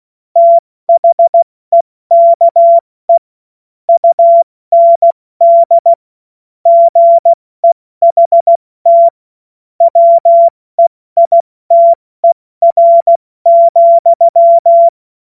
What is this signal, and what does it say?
Morse code, heard as a keyed tone, about 16 wpm, 680 Hz. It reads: THEKE UND GEHT WEITER,
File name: Morsekurs / Witze / Witz 15. November 2022 70 F80